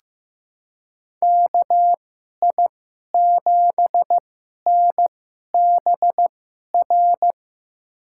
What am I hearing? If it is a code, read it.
Morse code, heard as a keyed tone, about 15 wpm, spaced as for 11 wpm, 700 Hz. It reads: KI7NBR
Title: USA Call Signs - 6 Characters - 2x Character Spacing - 15wpm